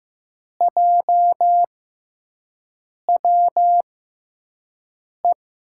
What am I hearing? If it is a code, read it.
Morse code, heard as a keyed tone, about 15 wpm, spaced as for 5 wpm, 700 Hz. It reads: JWE
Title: USA Call Signs - 6 Characters - 6x Character Spacing - 15wpm